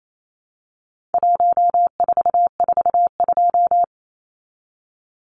Morse code, heard as a keyed tone, about 28 wpm, 700 Hz. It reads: 1442
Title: Numbers - 4-Digit 28wpm